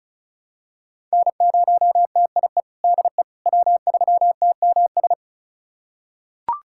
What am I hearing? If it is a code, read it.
Morse code, heard as a keyed tone, about 36 wpm, 700 Hz. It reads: N0TIE DE W3TMS
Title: USA Callsign DE Callsign - 5 Characters - 35wpm